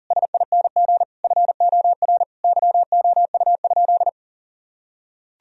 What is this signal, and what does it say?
Morse code, heard as a keyed tone, about 40 wpm, 700 Hz. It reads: SING FOR YOU?
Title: Sentences from Top 1000 Words - Review Part 2 - 40wpm